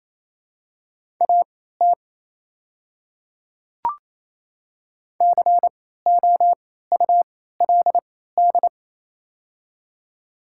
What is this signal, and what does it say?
Morse code, heard as a keyed tone, about 28 wpm, spaced as for 16 wpm, 700 Hz. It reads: AT COULD
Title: 3x Character Spacing - Top 100 Words - 28wpm